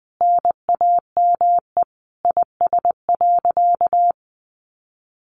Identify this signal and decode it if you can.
Morse code, heard as a keyed tone, about 19 wpm, 700 Hz. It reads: NAME IS.
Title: Sentences from Top 1100 Words 20wpm